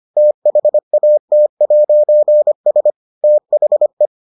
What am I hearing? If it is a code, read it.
Morse code, heard as a keyed tone, about 25 wpm, 600 Hz. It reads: THAT'S THE